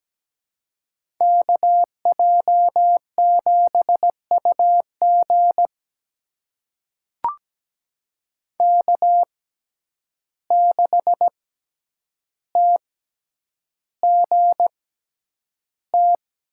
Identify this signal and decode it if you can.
Morse code, heard as a keyed tone, about 17 wpm, spaced as for 9 wpm, 700 Hz. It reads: KJ7UG K6TGT
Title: USA Call Signs - 5 Characters - 6x Character Spacing - 17wpm